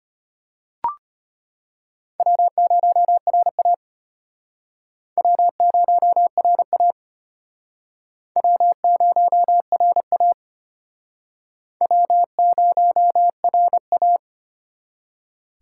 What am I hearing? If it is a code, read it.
Morse code, heard as a keyed tone, about 34 wpm, 700 Hz. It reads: W0RA W0RA W0RA W0RA